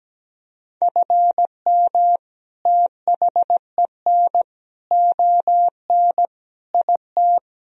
Morse code, heard as a keyed tone, about 17 wpm, 700 Hz. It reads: FM THEN ON IT
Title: Sentences from Top 400 Words and Abbreviations 17wpm